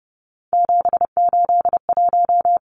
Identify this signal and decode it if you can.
Morse code, heard as a keyed tone, about 30 wpm, 700 Hz. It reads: 781
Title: Numbers - 3-Digit 30wpm